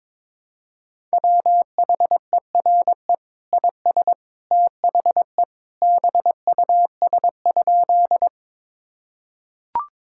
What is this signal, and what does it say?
Morse code, heard as a keyed tone, about 22 wpm, 700 Hz. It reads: WHERE IS THE BUS?
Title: Sentences from Top 1000 Words - Review Part 6 - 22wpm